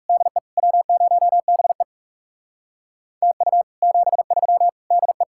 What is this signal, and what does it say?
Morse code, heard as a keyed tone, about 37 wpm, 700 Hz. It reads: DE W0BE TU 73 DE
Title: Speed-Racing Common QSO Phrases - 30wpm